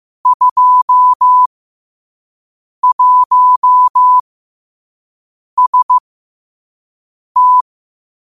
Morse code, heard as a keyed tone, about 15 wpm, spaced as for 5 wpm, 1000 Hz. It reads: 21ST
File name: qotdfiles / 5wpm